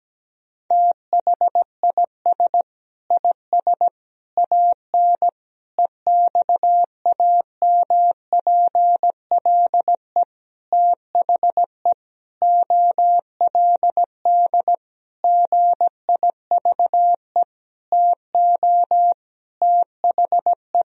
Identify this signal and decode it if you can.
Morse code, heard as a keyed tone, about 17 wpm, 700 Hz. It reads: THIS IS AN EXAMPLE THE OLD GIVE TO THE
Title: Sentences from Top 1000 Words - Review Part 1 - 17wpm